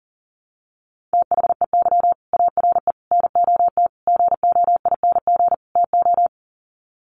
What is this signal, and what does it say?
Morse code, heard as a keyed tone, about 40 wpm, 700 Hz. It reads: THEY ARE NOT GOING TO
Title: Sets of 5 Words Encore2 40wpm